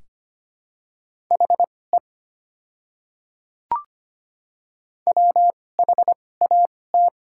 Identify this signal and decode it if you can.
Morse code, heard as a keyed tone, about 25 wpm, spaced as for 18 wpm, 700 Hz. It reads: HE WHAT
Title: 2x Character Spacing - Top 100 Words - 25wpm